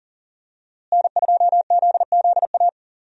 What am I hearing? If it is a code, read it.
Morse code, heard as a keyed tone, about 40 wpm, 700 Hz. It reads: N2ZZA
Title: USA Call Signs - 5 Characters - Repeated 2x - 40wpm